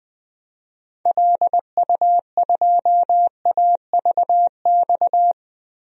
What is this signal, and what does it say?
Morse code, heard as a keyed tone, about 20 wpm, 700 Hz. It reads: LU2AVX
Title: DXCC-100 Call Signs - 6 Characters - 20wpm